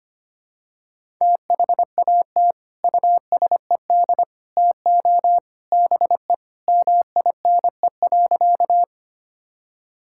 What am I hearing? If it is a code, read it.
Morse code, heard as a keyed tone, about 25 wpm, 700 Hz. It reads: THAT USED TO BE MINE.
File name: practice / 25wpm / Sentences with 3-Word Phrases - 25wpm